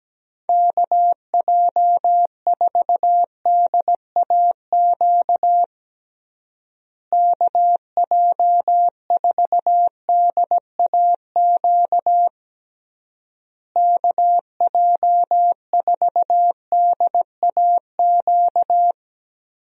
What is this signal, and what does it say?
Morse code, heard as a keyed tone, about 17 wpm, 700 Hz. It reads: KJ4DAQ KJ4DAQ KJ4DAQ
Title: USA Call Signs - 6 Characters - Repeated 3x - 17wpm